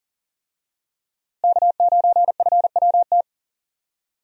Morse code, heard as a keyed tone, about 39 wpm, 700 Hz. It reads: K9FWT
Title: USA Call Signs - 5 Characters - Repeated 3x - 40wpm